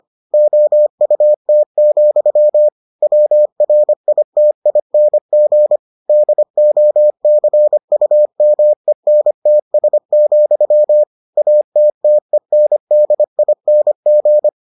Morse code, heard as a keyed tone, about 25 wpm, 600 Hz. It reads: OUT, WRITING DOCUMENTS, ATTENDING